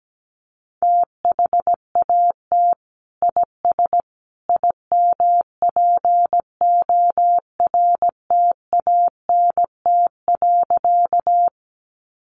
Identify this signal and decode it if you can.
Morse code, heard as a keyed tone, about 17 wpm, 700 Hz. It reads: THAT IS IMPORTANT.
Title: Sentences from Top 600 Words 17wpm